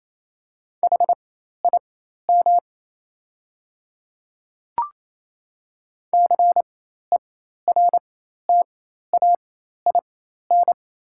Morse code, heard as a keyed tone, about 28 wpm, spaced as for 13 wpm, 700 Hz. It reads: HIM CERTAIN
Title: 4x Character Spacing - Top 500 Words - 28wpm